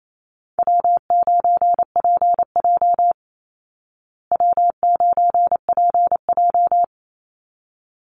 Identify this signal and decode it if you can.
Morse code, heard as a keyed tone, about 28 wpm, 700 Hz. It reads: W9PJ W9PJ
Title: Easy Call Signs Repeated 2x - 28wpm